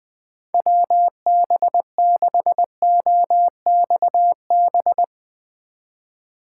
Morse code, heard as a keyed tone, about 20 wpm, 700 Hz. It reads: WB6OXB
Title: Speed Racing - USA 2x3 Call Signs 20wpm